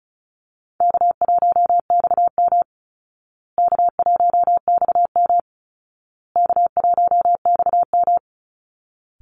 Morse code, heard as a keyed tone, about 34 wpm, 700 Hz. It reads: K1XM K1XM K1XM
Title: Call Signs Repeated 2x - 35wpm